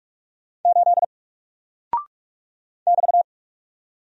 Morse code, heard as a keyed tone, about 44 wpm, 700 Hz. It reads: 8 <BT>
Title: Single Character (Letter:Number:Punctuation:Prosign) Alt 45wpm